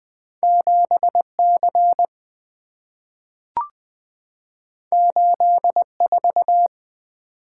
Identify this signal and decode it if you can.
Morse code, heard as a keyed tone, about 20 wpm, 690 Hz. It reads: 7C 84
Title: Two Letter-Number Combinations - 20wpm